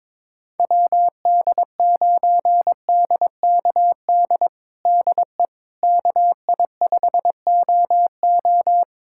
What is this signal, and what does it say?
Morse code, heard as a keyed tone, about 22 wpm, 700 Hz. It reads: WD9DKD DE KI5OO
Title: USA Callsign DE Callsign 22wpm